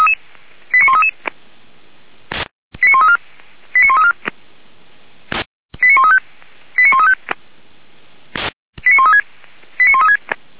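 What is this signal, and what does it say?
Five-tone selective call (ZVEI), heard as quick runs of five tones, tones of about 70 ms. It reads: 89133 89134 89134 89135 89135 89136 89136
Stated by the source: ZVEI five-tone call received by SDR, from feuerwehr-orlamuende.org (Orlamünde fire brigade).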